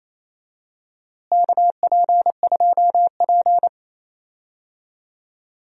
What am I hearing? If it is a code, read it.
Morse code, heard as a keyed tone, about 28 wpm, 700 Hz. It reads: KP2P
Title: DXCC-100 Call Signs - 4 Characters - Repeated 2x - 28wpm